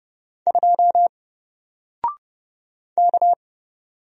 Morse code, heard as a keyed tone, about 30 wpm, 700 Hz. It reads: 2 K
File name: Single-Letter-Number-30wpm